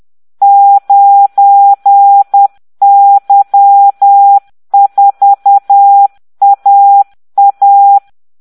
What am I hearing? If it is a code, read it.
Morse code, heard as a keyed tone, about 10 wpm, 800 Hz. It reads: 9Y4AA